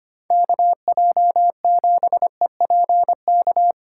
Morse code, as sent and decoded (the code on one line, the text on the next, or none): -.- .--- --... .--. -.-
KJ7EPK